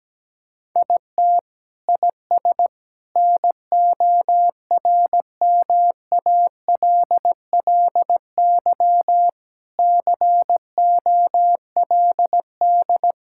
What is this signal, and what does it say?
Morse code, heard as a keyed tone, about 17 wpm, 700 Hz. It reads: IT IS NORMALLY COLD
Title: Sentences from Top 1000 Words - Review Part 6 - 17wpm